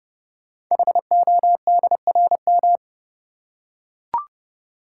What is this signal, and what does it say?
Morse code, heard as a keyed tone, about 30 wpm, 700 Hz. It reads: HODRM
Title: Code Groups - 5 Letters - 30wpm